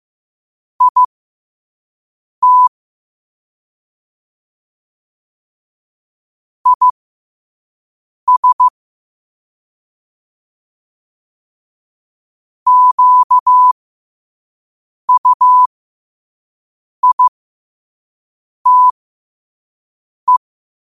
Morse code, heard as a keyed tone, about 15 wpm, spaced as for 5 wpm, 1000 Hz. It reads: IT IS QUITE
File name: qotdfiles / 5wpm